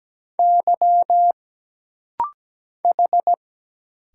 Morse code, heard as a keyed tone, about 17 wpm, 700 Hz. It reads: Y H
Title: Single Letter-Number Alt - Rapid-Fire 17wpm